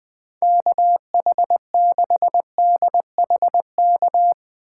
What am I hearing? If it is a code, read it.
Morse code, heard as a keyed tone, about 20 wpm, 700 Hz. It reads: KH6DHK
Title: K1USN SST QSO Practice - 20wpm